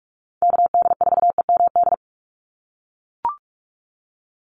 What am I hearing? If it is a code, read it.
Morse code, heard as a keyed tone, about 44 wpm, 700 Hz. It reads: KD4EMD
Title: USA Call Signs 45wpm